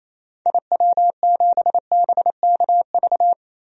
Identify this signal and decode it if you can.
Morse code, heard as a keyed tone, about 28 wpm, 700 Hz. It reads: IW7BKV
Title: DXCC-100 Call Signs - 6 Characters - Repeated 3x - 28wpm